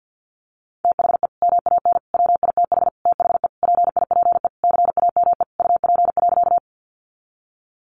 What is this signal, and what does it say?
Morse code, heard as a keyed tone, about 50 wpm, 700 Hz. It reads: THE MAN WITH THE PIPE CAME UP.
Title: Sentences from Top 1500 Words 50wpm